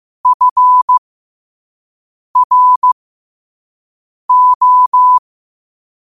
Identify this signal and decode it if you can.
Morse code, heard as a keyed tone, about 15 wpm, spaced as for 5 wpm, 1000 Hz. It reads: FRO
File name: qotdfiles / 5wpm